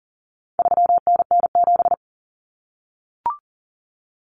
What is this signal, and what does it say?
Morse code, heard as a keyed tone, about 40 wpm, 700 Hz. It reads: 3NN7
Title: Common Cut Numbers - 4-Digit 40wpm